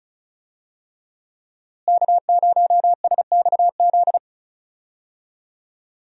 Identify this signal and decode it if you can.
Morse code, heard as a keyed tone, about 36 wpm, 700 Hz. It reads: K0SXZ
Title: USA Call Signs - 5 Characters - Repeated 3x - 35wpm